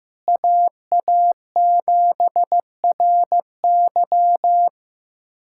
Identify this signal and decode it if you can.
Morse code, heard as a keyed tone, about 15 wpm, 700 Hz. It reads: AA7RY
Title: USA Callsign DE Callsign - 5 Characters - 15wpm